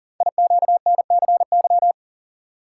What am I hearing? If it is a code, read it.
Morse code, heard as a keyed tone, about 41 wpm, 700 Hz. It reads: IQNCY
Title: Code Groups - 5 Letters - 40wpm